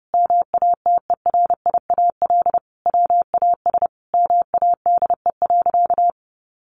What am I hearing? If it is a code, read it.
Morse code, heard as a keyed tone, about 30 wpm, 700 Hz. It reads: MATERIAL WAS MADE.
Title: Sentences from Top 1200 Words 30wpm